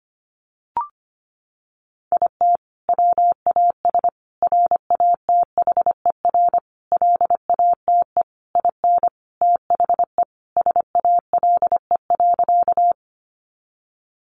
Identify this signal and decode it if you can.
Morse code, heard as a keyed tone, about 25 wpm, 700 Hz. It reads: IT WAS RATHER LATE IN THE SALE.
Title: Sentences from Top 1400 Words 25wpm